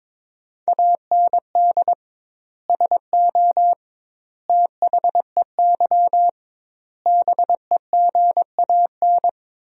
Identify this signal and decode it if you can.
Morse code, heard as a keyed tone, about 21 wpm, 700 Hz. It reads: AND SO THEY BEGAN
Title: 2x Word-Spacing - Sentences from Top 400 Words - 22wpm